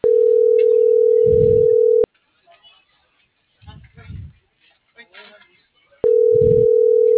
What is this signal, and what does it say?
Ringback tone (North America), 440 + 480 Hz.